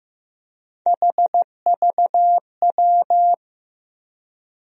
Morse code, heard as a keyed tone, about 15 wpm, 700 Hz. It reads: HVW
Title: USA Call Sign Postfixes - 3 Letters 15wpm